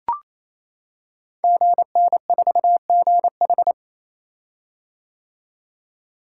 Morse code, heard as a keyed tone, about 28 wpm, 700 Hz. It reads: GN4GH
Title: Code Groups - 5 Letter:Numbers - 28wpm